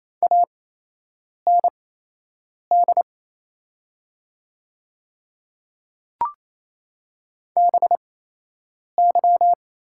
Morse code, heard as a keyed tone, about 28 wpm, spaced as for 8 wpm, 700 Hz. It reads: AND BY
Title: 8x Character Spacing - Top 200 Words - 28wpm